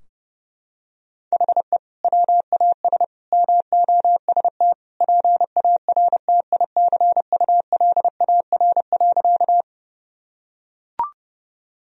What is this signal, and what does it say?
Morse code, heard as a keyed tone, about 32 wpm, 700 Hz. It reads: HE WAS MOST PARTICULAR.
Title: Sentences from Top 1000 Words - Review Part 4 - 30wpm